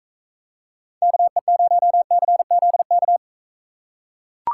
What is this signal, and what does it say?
Morse code, heard as a keyed tone, about 40 wpm, 700 Hz. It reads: KE0CZK